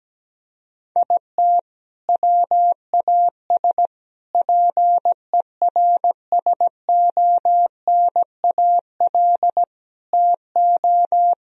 Morse code, heard as a keyed tone, about 17 wpm, 700 Hz. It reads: IT WAS PERSONAL TO
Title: Sentences from Top 1000 Words - Review Part 4 - 17wpm